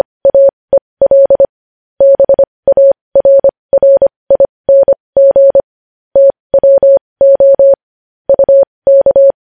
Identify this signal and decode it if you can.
Morse code, heard as a keyed tone, about 25 wpm, spaced as for 20 wpm, 550 Hz. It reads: AEL BARRING TWO UK
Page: news headlines in morse code at 20 wpm